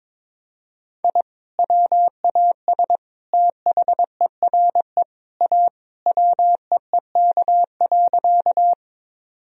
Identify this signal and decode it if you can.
Morse code, heard as a keyed tone, about 22 wpm, 700 Hz. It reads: I WAS THERE A WEEK.
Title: Sentences from Top 1000 Words - Review Part 2 - 22wpm